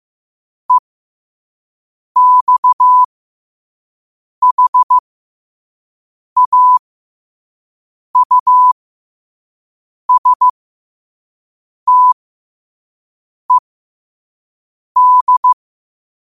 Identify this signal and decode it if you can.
Morse code, heard as a keyed tone, about 15 wpm, spaced as for 5 wpm, 1000 Hz. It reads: EXHAUSTED
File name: qotdfiles / 5wpm